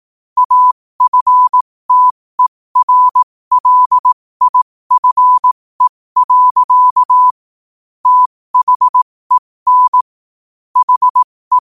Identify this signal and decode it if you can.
Morse code, heard as a keyed tone, about 18 wpm, spaced as for 15 wpm, 1000 Hz. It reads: AFTERLIFE. THEN HE
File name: qotdfiles / 15wpm